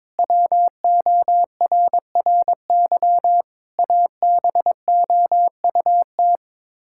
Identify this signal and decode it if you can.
Morse code, heard as a keyed tone, about 22 wpm, 700 Hz. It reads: WORRY ABOUT
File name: practice / 22wpm / Sets of 6 Words - 22wpm